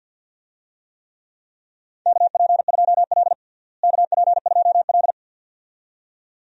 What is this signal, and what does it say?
Morse code, heard as a keyed tone, about 51 wpm, 700 Hz. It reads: KP2L KP2L
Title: DXCC-100 Call Signs - 4 Characters - Repeated 2x - 50wpm